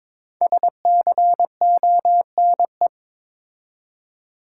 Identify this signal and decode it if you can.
Morse code, heard as a keyed tone, about 22 wpm, 700 Hz. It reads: SCONE
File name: practice / 22wpm / Difficult Words with Hints - 22wpm